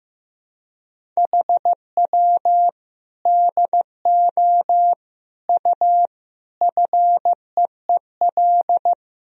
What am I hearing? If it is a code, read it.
Morse code, heard as a keyed tone, about 15 wpm, 700 Hz. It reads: HW DO U FEEL
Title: Sentences from Top 400 Words and Abbreviations 15wpm